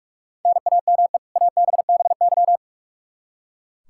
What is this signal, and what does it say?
Morse code, heard as a keyed tone, about 45 wpm, 700 Hz. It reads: NAME ABBY
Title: Practice QSO - Exchange 1 - Beginning - 45wpm